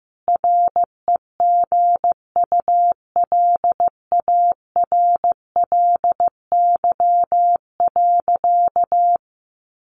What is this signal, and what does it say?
Morse code, heard as a keyed tone, about 15 wpm, 700 Hz. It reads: REGULARLY.
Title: Sentences from Top 1400 Words 15wpm